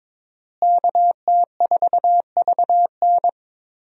Morse code, heard as a keyed tone, about 22 wpm, 700 Hz. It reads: KT4VN
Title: USA Call Signs - 5 Characters - Repeated 2x - 22wpm